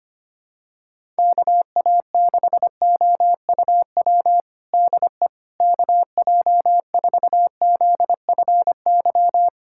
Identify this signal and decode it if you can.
Morse code, heard as a keyed tone, about 25 wpm, 700 Hz. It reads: KA6OUW DE KJ4ZFY
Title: USA Callsign DE Callsign - 6 Characters - 25wpm